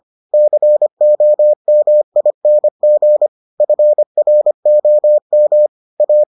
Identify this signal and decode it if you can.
Morse code, heard as a keyed tone, about 25 wpm, 600 Hz. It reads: COMING FROM A